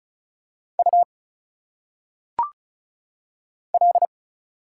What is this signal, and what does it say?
Morse code, heard as a keyed tone, about 36 wpm, 700 Hz. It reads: U L
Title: Single Letters - Phonetic Pronunciation 35wpm